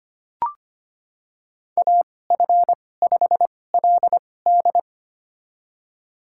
Morse code, heard as a keyed tone, about 25 wpm, spaced as for 18 wpm, 700 Hz. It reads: AF5LD
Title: USA Call Signs - 2x Character Spacing - 25wpm